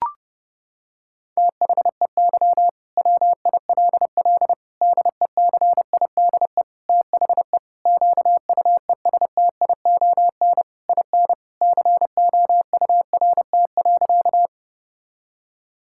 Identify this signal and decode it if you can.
Morse code, heard as a keyed tone, about 30 wpm, 700 Hz. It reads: THEY WILL DECIDE THE QUESTION IN COURT.
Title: Sentences from Top 1600 Words 30wpm